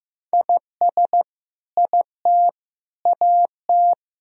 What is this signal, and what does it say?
Morse code, heard as a keyed tone, about 15 wpm, 700 Hz. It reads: IS IT AT
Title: Sentences from Top 1000 Words - Review Part 4 - 15wpm